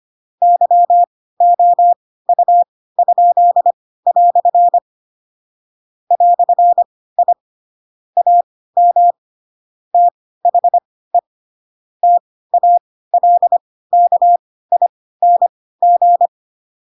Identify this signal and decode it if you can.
Morse code, heard as a keyed tone, about 25 wpm, spaced as for 16 wpm, 700 Hz. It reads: YOU?" "I AM THE TALKING
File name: Pinocchio / Pinocchio ch4 10 25 ik1hge